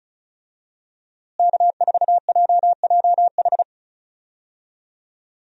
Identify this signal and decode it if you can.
Morse code, heard as a keyed tone, about 36 wpm, 700 Hz. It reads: K4JJH
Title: USA Call Signs - 5 Characters - Repeated 2x - 35wpm